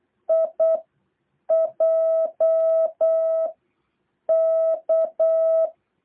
Morse code, heard as a keyed tone, about 8 wpm, 650 Hz. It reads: I J K
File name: jh8cbh / 2023cw